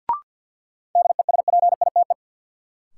Morse code, heard as a keyed tone, about 53 wpm, 700 Hz. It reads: DESPITE
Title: Sentences from Top 1000 Words - Review Part 7 - New Word Drill - 50wpm